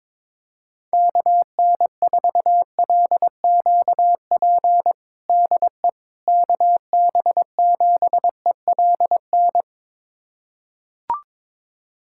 Morse code, heard as a keyed tone, about 22 wpm, 700 Hz. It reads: KN4LQP DE KB7ELN
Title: USA Callsign DE Callsign 22wpm